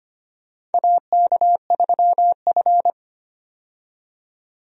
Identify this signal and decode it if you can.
Morse code, heard as a keyed tone, about 25 wpm, 700 Hz. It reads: AK3F
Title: USA Call Signs - 4 Characters - 25wpm